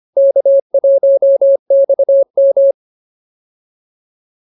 Morse code, heard as a keyed tone, about 24 wpm, 550 Hz. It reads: K1XM
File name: QSO103 25